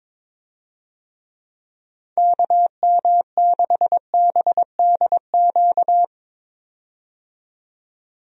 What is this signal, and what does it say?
Morse code, heard as a keyed tone, about 22 wpm, 700 Hz. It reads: KM6BDQ
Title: USA Call Signs - 6 Characters - 22wpm